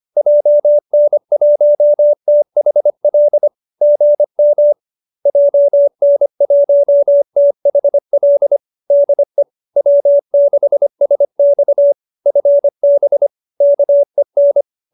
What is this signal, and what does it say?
Morse code, heard as a keyed tone, about 25 wpm, 590 Hz. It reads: JN1THL GM JN1THL DE W6SX FB KEN